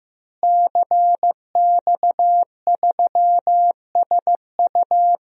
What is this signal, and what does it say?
Morse code, heard as a keyed tone, about 14 wpm, 700 Hz. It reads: CX3SU
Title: DXCC-100 Call Signs - 5 Characters - Repeated 3x - 15wpm